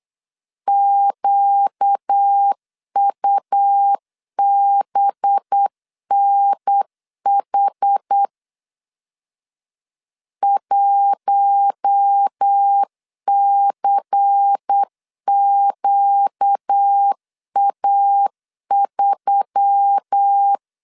Morse code, heard as a keyed tone, about 9 wpm, 800 Hz. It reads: QUBNH 1CQA3
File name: CW / DET / 1 6 1500C S A 8wpm 5203